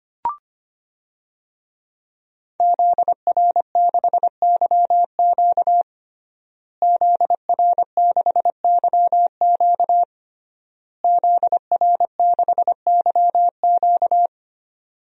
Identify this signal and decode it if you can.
Morse code, heard as a keyed tone, about 25 wpm, 700 Hz. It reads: ZR6YQ ZR6YQ ZR6YQ
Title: DXCC-100 Call Signs - 5 Characters - Repeated 3x - 25wpm